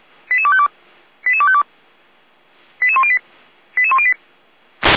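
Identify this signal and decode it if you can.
Five-tone selective call (ZVEI), heard as quick runs of five tones, tones of about 70 ms. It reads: 88252 88252 88108 88108
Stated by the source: ZVEI five-tone call received with an SDR, from the Feuerwehr Orlamünde audio server.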